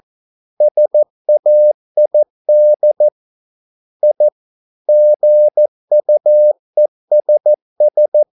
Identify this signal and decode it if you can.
Morse code, heard as a keyed tone, about 14 wpm, 600 Hz. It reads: SAID I GUESS